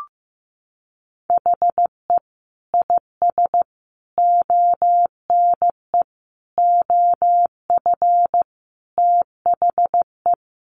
Morse code, heard as a keyed tone, about 15 wpm, 700 Hz. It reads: HE IS ONE OF THE
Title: Sets of 5 Words Encore 15wpm